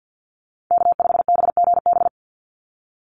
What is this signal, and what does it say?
Morse code, heard as a keyed tone, about 50 wpm, 700 Hz. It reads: K5BGB
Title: Call Signs - 50wpm